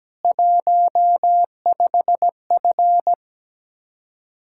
Morse code, heard as a keyed tone, about 18 wpm, 700 Hz. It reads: 15F
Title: Speed-Racing Common QSO Phrases - 17wpm